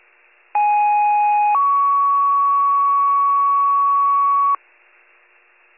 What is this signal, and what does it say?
Two-tone page, heard as a one-second tone, then a three-second tone, about 830 then 1120 Hz. It is heard from 0.5 to 4.6 s.